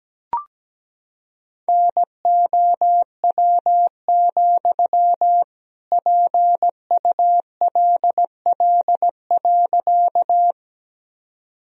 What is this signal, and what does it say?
Morse code, heard as a keyed tone, about 17 wpm, 700 Hz. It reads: NOW, PULL.
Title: Sentences from Top 1000 Words - Review Part 2 - 17wpm